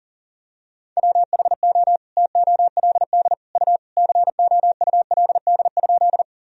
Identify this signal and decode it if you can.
Morse code, heard as a keyed tone, about 40 wpm, 700 Hz. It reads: WHO TOLD U COULD?
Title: Sentences from Top 400 Words and Abbreviations 40wpm